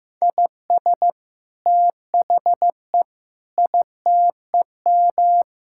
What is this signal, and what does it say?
Morse code, heard as a keyed tone, about 14 wpm, 700 Hz. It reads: IS THE ITEM
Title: Sentences from Top 1000 Words - Review Part 6 - 15wpm